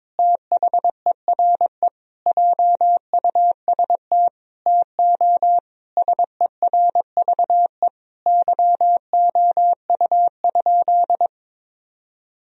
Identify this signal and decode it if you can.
Morse code, heard as a keyed tone, about 22 wpm, 700 Hz. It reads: THERE JUST TO SERVE YOU?
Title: Sentences from Top 1000 Words - Review Part 2 - 22wpm